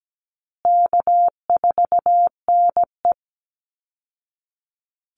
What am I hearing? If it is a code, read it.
Morse code, heard as a keyed tone, about 17 wpm, 700 Hz. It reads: K4NE
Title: Call Signs Repeated 3x - 17wpm